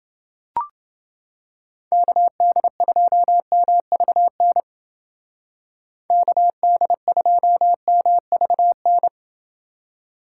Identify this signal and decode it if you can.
Morse code, heard as a keyed tone, about 29 wpm, 700 Hz. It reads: KD2MVN KD2MVN